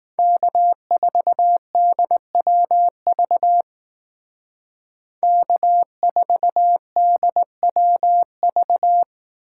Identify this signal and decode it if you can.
Morse code, heard as a keyed tone, about 18 wpm, 700 Hz. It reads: K4DWV K4DWV
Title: Speed Racing - USA 1x3 Call Signs 15wpm